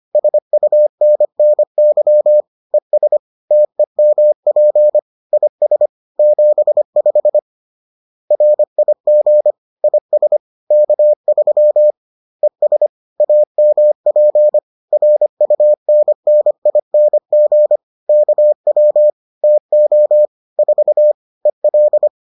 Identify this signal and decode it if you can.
Morse code, heard as a keyed tone, about 25 wpm, 600 Hz. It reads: SUNNY ES TEMP IS 75 RIG IS K3 ES AMP RUNNING KW TO 4 EL